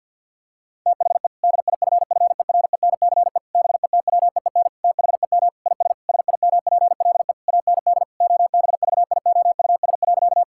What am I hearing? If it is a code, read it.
Morse code, heard as a keyed tone, about 50 wpm, 700 Hz. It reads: THE DIFFERENCE BETWEEN THEM IS SIMPLE AND OBVIOUS.
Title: Sentences from Top 1600 Words 50wpm